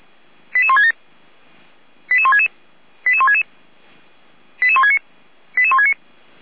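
Five-tone selective call (ZVEI), heard as quick runs of five tones, tones of about 70 ms. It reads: 88167 88166 88166 88169 88169
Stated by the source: Five-tone ZVEI selective call received by SDR.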